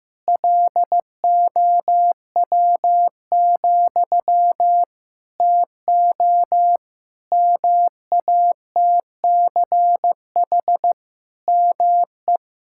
Morse code, heard as a keyed tone, about 15 wpm, 700 Hz. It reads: LOW, TO MATCH ME
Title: Sentences from Top 1600 Words 15wpm